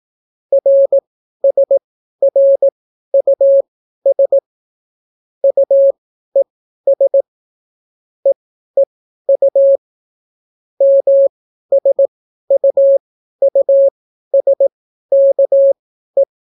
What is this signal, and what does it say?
Morse code, heard as a keyed tone, about 18 wpm, spaced as for 12 wpm, 550 Hz. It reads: RSRUS UES EEU MSUUSKE